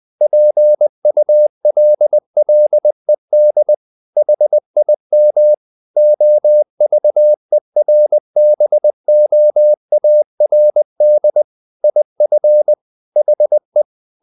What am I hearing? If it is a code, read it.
Morse code, heard as a keyed tone, about 20 wpm, 600 Hz. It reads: PULLED HIM OVERBOARD IF HE
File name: the old man and the sea / the old man and the sea part 1 20wpm